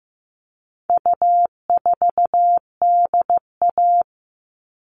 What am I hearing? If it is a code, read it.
Morse code, heard as a keyed tone, about 15 wpm, 700 Hz. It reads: U4DA
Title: All Cut Numbers - 4-Digit 15wpm